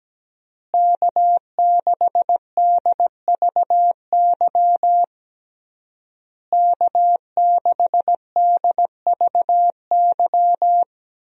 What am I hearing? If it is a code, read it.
Morse code, heard as a keyed tone, about 17 wpm, 700 Hz. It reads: K6DVY K6DVY
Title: USA Call Signs - 5 Characters - Repeated 3x - 17wpm